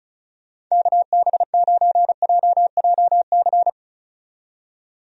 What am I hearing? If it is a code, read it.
Morse code, heard as a keyed tone, about 36 wpm, 700 Hz. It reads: KB9JJC
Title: USA Call Signs - 6 Characters - 35wpm